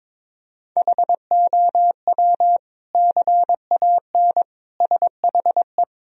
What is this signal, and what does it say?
Morse code, heard as a keyed tone, about 22 wpm, 700 Hz. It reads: HOW CAN SHE
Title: Sentences from Top 1000 Words - Review Part 2 - 22wpm